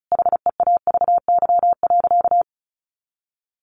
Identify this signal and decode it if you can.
Morse code, heard as a keyed tone, about 34 wpm, 700 Hz. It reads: HEAVY.
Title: Sentences from Top 1100 Words 35wpm